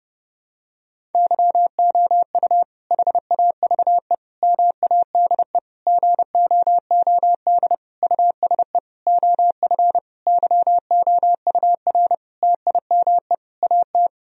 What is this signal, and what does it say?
Morse code, heard as a keyed tone, about 32 wpm, 700 Hz. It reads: YOU HAVE MADE GOOD USE OF YOUR TIME AT
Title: Sentences from Top 1000 Words - Review Part 1 - 30wpm